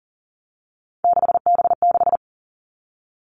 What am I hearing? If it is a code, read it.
Morse code, heard as a keyed tone, about 40 wpm, 700 Hz. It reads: 6B6
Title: All Cut Numbers - 3-Digit 40wpm